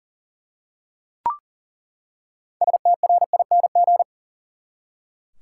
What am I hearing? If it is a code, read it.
Morse code, heard as a keyed tone, about 40 wpm, 700 Hz. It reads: STRING
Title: Six Letter Words - 40wpm